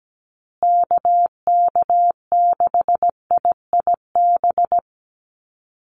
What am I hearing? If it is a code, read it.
Morse code, heard as a keyed tone, about 17 wpm, 700 Hz. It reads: KK6IIB
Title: USA Call Signs Repeated 2x 17wpm